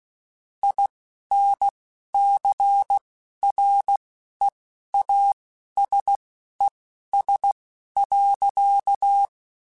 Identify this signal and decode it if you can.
Morse code, heard as a keyed tone, about 16 wpm, spaced as for 11 wpm, 800 Hz. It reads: INCREASES.
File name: Technical Subjects / Technical Subjects-mp3s / XC @10